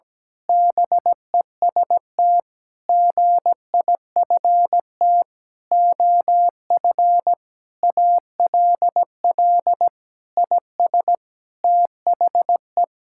Morse code, heard as a keyed tone, about 17 wpm, 700 Hz. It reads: BEST GIFT OF ALL IS THE
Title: Sentences from Top 1000 Words - Review Part 6 - 17wpm